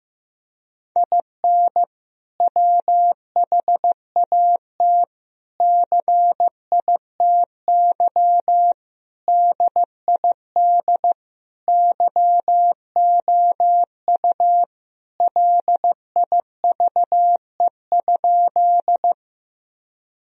Morse code, heard as a keyed tone, about 15 wpm, 700 Hz. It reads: IN WHAT CITY DID YOU LIVE?
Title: Sentences from Top 1000 Words - Review Part 1 - 15wpm